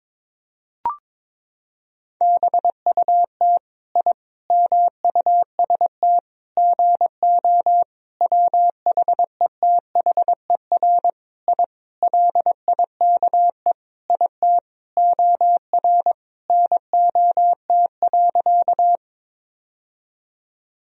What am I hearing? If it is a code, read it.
Morse code, heard as a keyed tone, about 22 wpm, 700 Hz. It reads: BUT I MUST GO WHETHER I LIKE IT OR NOT.
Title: Sentences from Top 1000 Words - Review Part 3 - 22wpm